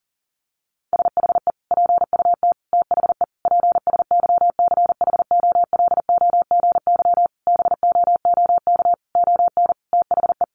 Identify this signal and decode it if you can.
Morse code, heard as a keyed tone, about 37 wpm, 700 Hz. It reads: SHE PUT THE PSYCHOLOGY BOOK ON THE
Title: Sentences from Top 1400 Words 40wpm